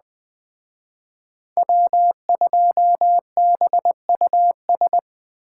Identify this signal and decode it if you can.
Morse code, heard as a keyed tone, about 20 wpm, 700 Hz. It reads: W2BUS